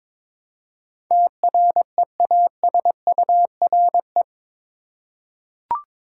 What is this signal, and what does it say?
Morse code, heard as a keyed tone, about 22 wpm, 700 Hz. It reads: TREASURE
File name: practice / 22wpm / Eight Letter Words - 22wpm